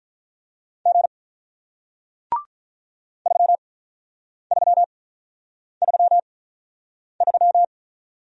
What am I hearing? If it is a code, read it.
Morse code, heard as a keyed tone, about 46 wpm, spaced as for 9 wpm, 700 Hz. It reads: G 3333